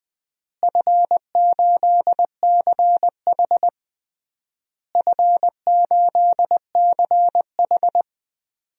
Morse code, heard as a keyed tone, about 20 wpm, 700 Hz. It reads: F8CH F8CH
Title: DXCC-100 Call Signs - 4 Characters - Repeated 3x - 20wpm